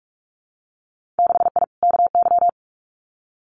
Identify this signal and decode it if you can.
Morse code, heard as a keyed tone, about 44 wpm, 700 Hz. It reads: 6I KY
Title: Winter Field Day Exchanges 45wpm